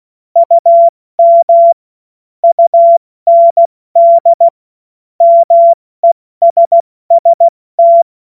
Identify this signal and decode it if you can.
Morse code, heard as a keyed tone, about 16 wpm, spaced as for 14 wpm, 680 Hz. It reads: UM UND MESST
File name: Morsekurs / Witze / Witz 10.Oktober 2023 70 F80